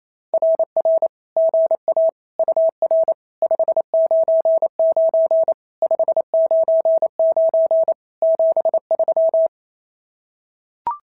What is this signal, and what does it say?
Morse code, heard as a keyed tone, about 28 wpm, 650 Hz. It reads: RR GA UR 599 599 73
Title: SOTA QSO Practice 28wpm